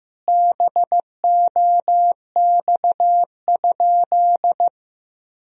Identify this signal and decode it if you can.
Morse code, heard as a keyed tone, about 15 wpm, 700 Hz. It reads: BOX?
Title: Sentences from Top 1000 Words - Review Part 2 - 15wpm